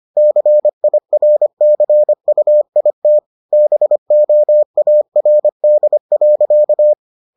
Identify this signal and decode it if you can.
Morse code, heard as a keyed tone, about 25 wpm, 600 Hz. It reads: CIRCUIT BOARD.